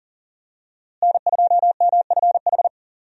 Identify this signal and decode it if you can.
Morse code, heard as a keyed tone, about 40 wpm, 700 Hz. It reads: N2MFH
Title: USA Call Signs - 5 Characters - Repeated 2x - 40wpm